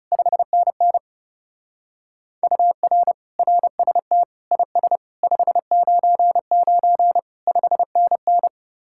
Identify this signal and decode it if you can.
Morse code, heard as a keyed tone, about 37 wpm, 700 Hz. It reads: 5NN UR RST IS 599 5NN
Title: Speed-Racing Common QSO Phrases - 30wpm